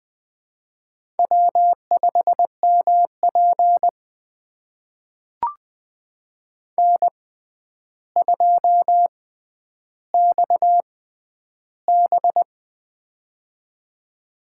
Morse code, heard as a keyed tone, about 20 wpm, spaced as for 10 wpm, 700 Hz. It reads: W5MP N2XB
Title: USA Call Signs - 4 Characters - 6x Character Spacing - 20wpm